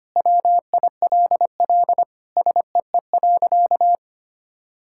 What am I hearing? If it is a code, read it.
Morse code, heard as a keyed tone, about 25 wpm, 700 Hz. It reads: WILL SEE.
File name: practice / 25wpm / Sentences with 3-Word Phrases - 25wpm